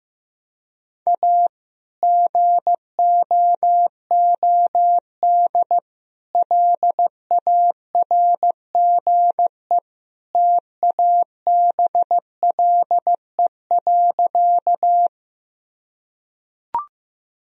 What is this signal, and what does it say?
Morse code, heard as a keyed tone, about 15 wpm, 700 Hz. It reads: A GOOD LARGE TABLE.
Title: Sentences from Top 1000 Words - Review Part 2 - 15wpm